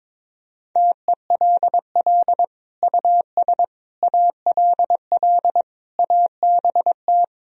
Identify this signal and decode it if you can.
Morse code, heard as a keyed tone, about 21 wpm, 700 Hz. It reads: TELL US ALL ABT
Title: Sentences from Top 200 Words and Abbreviations 22wpm